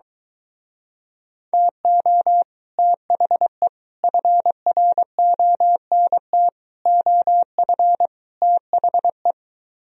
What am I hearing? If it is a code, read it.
Morse code, heard as a keyed tone, about 23 wpm, 700 Hz. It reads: TO THE FRONT OF THE